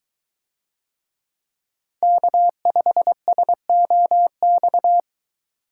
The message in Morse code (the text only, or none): K5SOX